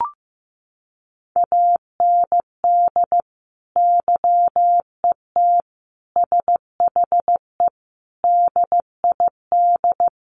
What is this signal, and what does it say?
Morse code, heard as a keyed tone, about 15 wpm, 700 Hz. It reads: AND YET SHE DID
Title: Sentences from Top 1500 Words 15wpm